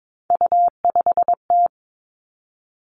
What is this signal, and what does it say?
Morse code, heard as a keyed tone, about 22 wpm, 700 Hz. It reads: U5T
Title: Three Challenging Letter-Number Combinations 22wpm